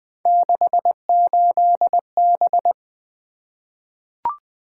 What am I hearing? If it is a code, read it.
Morse code, heard as a keyed tone, about 20 wpm, 700 Hz. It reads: 68B
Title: Three Letter-Number Combinations - 20wpm